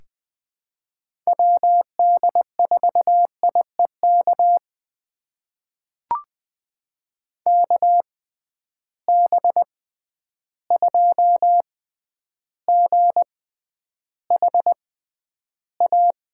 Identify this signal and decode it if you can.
Morse code, heard as a keyed tone, about 20 wpm, spaced as for 10 wpm, 700 Hz. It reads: WD4IEK KB2GHA
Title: USA Call Signs - 6 Characters - 6x Character Spacing - 20wpm